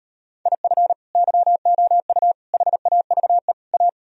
Morse code, heard as a keyed tone, about 38 wpm, 700 Hz. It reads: IF YOU HAVE A